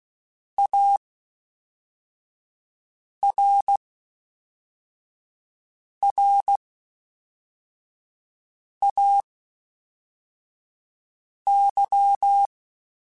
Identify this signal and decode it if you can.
Morse code, heard as a keyed tone, about 16 wpm, spaced as for 4 wpm, 800 Hz. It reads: ARRAY
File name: (spaced) / FromSMs spaced mp3s / Antennas sp@5